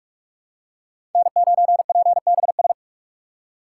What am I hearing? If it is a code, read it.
Morse code, heard as a keyed tone, about 48 wpm, 700 Hz. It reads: N9PBS
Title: USA Call Signs - 5 Characters - 6x Character Spacing - 45wpm